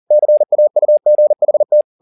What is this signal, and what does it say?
Morse code, heard as a keyed tone, about 40 wpm, 600 Hz. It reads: CAUGHT
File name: the old man and the sea 2 40wpm